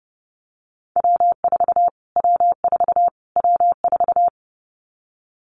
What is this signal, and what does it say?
Morse code, heard as a keyed tone, about 28 wpm, 700 Hz. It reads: W4 W4 W4
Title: Callsign Prefixes for Top 25 DXCC - Repeated 3x - 30wpm